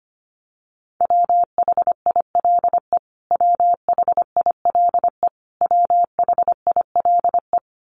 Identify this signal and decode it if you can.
Morse code, heard as a keyed tone, about 25 wpm, 700 Hz. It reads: WHILE WHILE WHILE
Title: Top 300 Words Repeated 25wpm